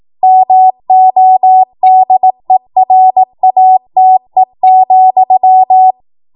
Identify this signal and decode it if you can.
Morse code, heard as a keyed tone, about 18 wpm, 750 Hz. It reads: MODERATE,
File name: Morse / 18WPM